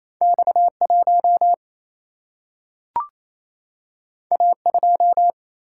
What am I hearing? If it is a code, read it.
Morse code, heard as a keyed tone, about 28 wpm, 700 Hz. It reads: X1 A2
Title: Single-Letter Single-Number 28wpm